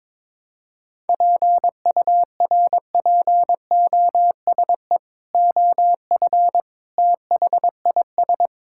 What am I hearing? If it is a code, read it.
Morse code, heard as a keyed tone, about 22 wpm, 700 Hz. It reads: PURPOSE OF THIS
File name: practice / 22wpm / Build A Phrase - 5-Word Phrases - 22wpm